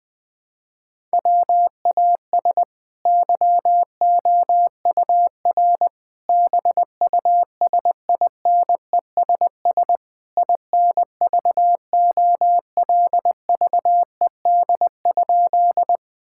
Morse code, heard as a keyed tone, about 20 wpm, 700 Hz. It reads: WAS YOUR BUSINESS INVOLVED?
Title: Sentences from Top 1000 Words - Review Part 5 - 20wpm